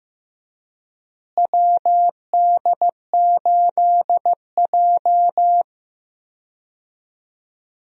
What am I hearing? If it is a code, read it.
Morse code, heard as a keyed tone, about 15 wpm, 700 Hz. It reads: WD8J
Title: USA Call Signs - 4 Characters - Repeated 3x - 15wpm